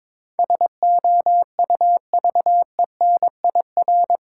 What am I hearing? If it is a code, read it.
Morse code, heard as a keyed tone, about 22 wpm, 700 Hz. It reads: SOUVENIR
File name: Difficult Words with Hints - 22wpm